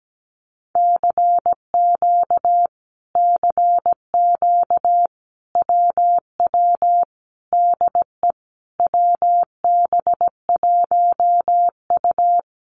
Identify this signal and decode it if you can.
Morse code, heard as a keyed tone, about 17 wpm, 700 Hz. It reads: CQ CQ WW DE WB1U
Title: Common QSO Phrases Encore 17wpm